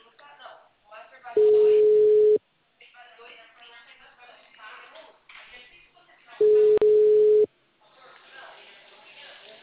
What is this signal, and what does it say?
Ringback tone (Europe, 425 Hz), with speech.